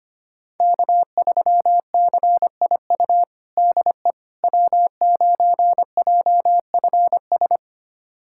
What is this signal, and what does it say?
Morse code, heard as a keyed tone, about 25 wpm, 700 Hz. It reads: K3CIU DE W9JFS
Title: USA Callsign DE Callsign - 5 Characters - 25wpm